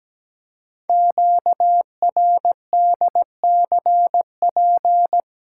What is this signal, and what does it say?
Morse code, heard as a keyed tone, about 17 wpm, 700 Hz. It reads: QRDCP
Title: Code Groups - 5 Letter:Numbers - 17wpm